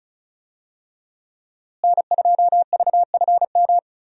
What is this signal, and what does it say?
Morse code, heard as a keyed tone, about 35 wpm, 700 Hz. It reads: N2VFM
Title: USA Call Signs - 5 Characters - Repeated 3x - 35wpm